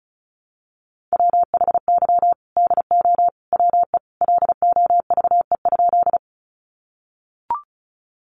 Morse code, heard as a keyed tone, about 34 wpm, 700 Hz. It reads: WHY DO WE LOVE?
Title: Sentences from Top 400 Words 35wpm